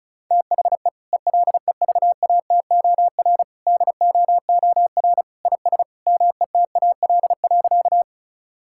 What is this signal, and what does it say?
Morse code, heard as a keyed tone, about 36 wpm, 700 Hz. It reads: THE ELEVATOR DOOR IS METAL.
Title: Sentences from Top 1600 Words 35wpm